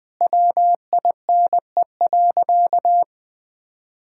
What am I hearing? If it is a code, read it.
Morse code, heard as a keyed tone, about 20 wpm, 700 Hz. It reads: WINE.
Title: All-Age Appropriate Jokes - 20wpm